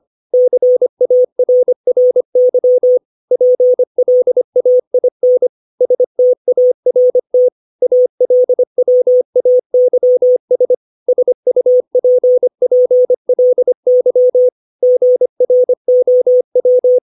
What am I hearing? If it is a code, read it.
Morse code, heard as a keyed tone, about 25 wpm, 500 Hz. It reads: CARRY PLAIN START ALWAYS SUPPLY GROW
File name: 632 words / 632 Words 25WPM